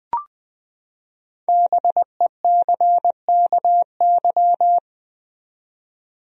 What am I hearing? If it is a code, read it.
Morse code, heard as a keyed tone, about 21 wpm, 700 Hz. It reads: BECKY
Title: USA Common Names - 20wpm